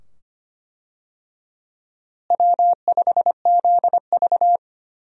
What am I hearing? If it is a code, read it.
Morse code, heard as a keyed tone, about 25 wpm, 700 Hz. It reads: W5ZV
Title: USA Call Signs - 4 Characters - Repeated 2x - 25wpm